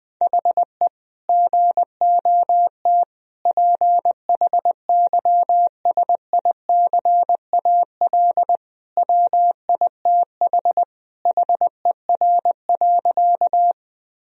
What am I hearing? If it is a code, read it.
Morse code, heard as a keyed tone, about 19 wpm, 700 Hz. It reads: HE GOT PHYSICAL WITH HER.
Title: Sentences from Top 1000 Words - Review Part 4 - 20wpm